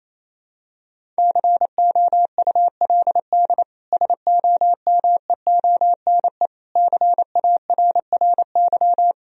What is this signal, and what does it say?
Morse code, heard as a keyed tone, about 28 wpm, 700 Hz. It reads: COULD SOMEONE CARRY
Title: Sentences from Top 1000 Words - Review Part 4 - 28wpm